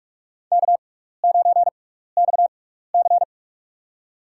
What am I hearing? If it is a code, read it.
Morse code, heard as a keyed tone, about 47 wpm, spaced as for 16 wpm, 700 Hz. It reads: K9XC
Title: USA Call Signs - 4 Characters - 6x Character Spacing - 45wpm